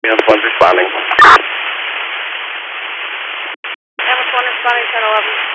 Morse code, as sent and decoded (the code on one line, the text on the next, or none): none